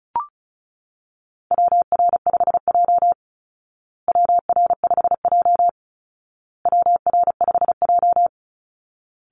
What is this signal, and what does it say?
Morse code, heard as a keyed tone, about 34 wpm, 700 Hz. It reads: WR5J WR5J WR5J
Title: Call Signs Repeated 2x - 35wpm